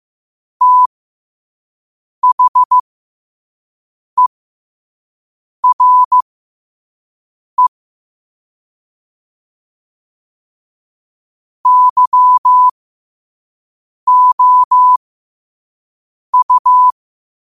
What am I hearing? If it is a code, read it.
Morse code, heard as a keyed tone, about 15 wpm, spaced as for 5 wpm, 1000 Hz. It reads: THERE YOU
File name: qotdfiles / 5wpm